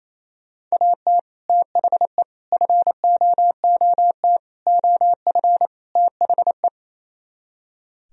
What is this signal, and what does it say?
Morse code, heard as a keyed tone, about 36 wpm, 700 Hz. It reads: AT THE FOOT OF THE